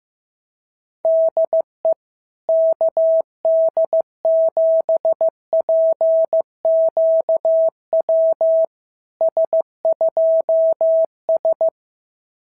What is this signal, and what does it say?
Morse code, heard as a keyed tone, about 15 wpm, 650 Hz. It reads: DE KD7PQW S2S
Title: SOTA QSO Practice 15wpm